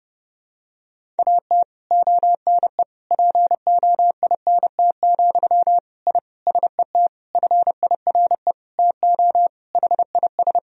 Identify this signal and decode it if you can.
Morse code, heard as a keyed tone, about 30 wpm, 700 Hz. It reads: AT ONE POINT, I SET FIRE TO HIS
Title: Sentences from Top 1000 Words - Review Part 1 - 30wpm